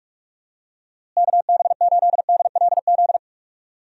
Morse code, heard as a keyed tone, about 46 wpm, 700 Hz. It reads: KB8DLZ
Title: USA Call Signs - 6 Characters - Repeated 3x - 45wpm